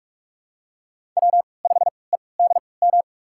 Morse code, heard as a keyed tone, about 43 wpm, spaced as for 25 wpm, 700 Hz. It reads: W5EDM
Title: USA Call Signs - 5 Characters - 3x Character Spacing - 45wpm